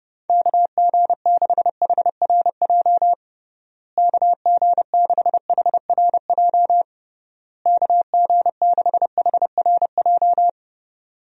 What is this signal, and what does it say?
Morse code, heard as a keyed tone, about 32 wpm, 700 Hz. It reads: KG6HRJ KG6HRJ KG6HRJ
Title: USA Call Signs - 6 Characters - Repeated 3x - 30wpm